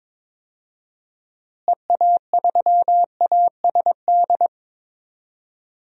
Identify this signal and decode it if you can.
Morse code, heard as a keyed tone, about 22 wpm, 700 Hz. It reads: EA3ASD
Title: DXCC-100 Call Signs - 6 Characters - Repeated 3x - 22wpm